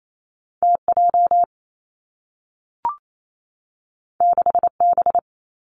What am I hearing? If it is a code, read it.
Morse code, heard as a keyed tone, about 28 wpm, 700 Hz. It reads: T J 6 B
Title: Two Challenging Letter-Number Combinations 28wpm